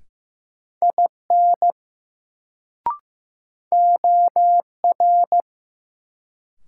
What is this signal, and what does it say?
Morse code, heard as a keyed tone, about 15 wpm, 700 Hz. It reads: IN OR
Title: Sentences from Top 100 Words - New Word Drill - 15wpm